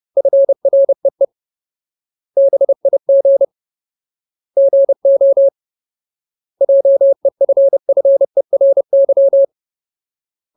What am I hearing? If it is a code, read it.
Morse code, heard as a keyed tone, about 30 wpm, 550 Hz. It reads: FREE BIG GO JEFFERY
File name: Ultimate Word List / 30WPM Ultimate